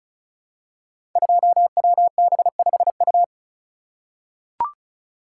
Morse code, heard as a keyed tone, about 36 wpm, 700 Hz. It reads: 2WB5U
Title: Code Groups - 5 Letter:Numbers - 35wpm